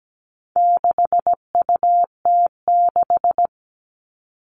Morse code, heard as a keyed tone, about 17 wpm, 700 Hz. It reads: 6UT6
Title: All Cut Numbers - 4-Digit 17wpm